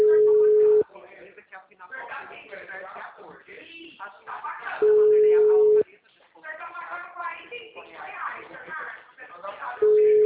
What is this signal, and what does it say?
Ringback tone (Europe, 425 Hz), with speech.